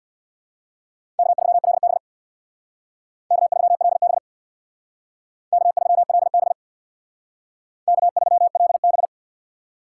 Morse code, heard as a keyed tone, about 62 wpm, 700 Hz. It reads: K3LB K3LB K3LB K3LB
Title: Speed Racing - USA 1x2 Call Signs 50wpm